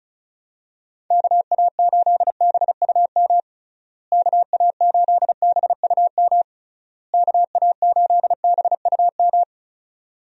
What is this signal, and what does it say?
Morse code, heard as a keyed tone, about 34 wpm, 700 Hz. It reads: KA8BUM KA8BUM KA8BUM
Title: USA Call Signs - 6 Characters - Repeated 3x - 35wpm